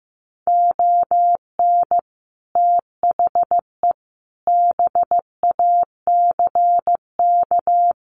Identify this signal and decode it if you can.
Morse code, heard as a keyed tone, about 15 wpm, 700 Hz. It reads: ON THE BACK
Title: Sets of 5 Words 15wpm